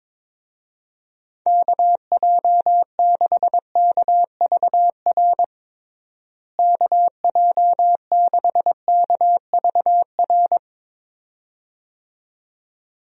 Morse code, heard as a keyed tone, about 22 wpm, 700 Hz. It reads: KJ6KVR KJ6KVR
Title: USA Call Signs - 6 Characters - Repeated 2x - 22wpm